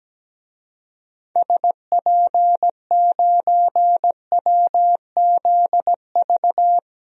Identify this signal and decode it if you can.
Morse code, heard as a keyed tone, about 17 wpm, 700 Hz. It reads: SP9WZV
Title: DXCC-100 Call Signs - 6 Characters - Repeated 2x - 17wpm